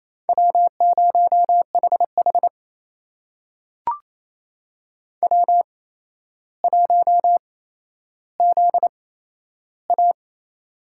Morse code, heard as a keyed tone, about 28 wpm, spaced as for 12 wpm, 700 Hz. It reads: W0HH W1ZA
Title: USA Call Signs - 4 Characters - 8x Character Spacing - 28wpm